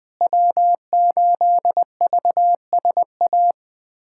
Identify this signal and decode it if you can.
Morse code, heard as a keyed tone, about 21 wpm, 700 Hz. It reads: W8VSA
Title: USA Callsign DE Callsign - 5 Characters - 20wpm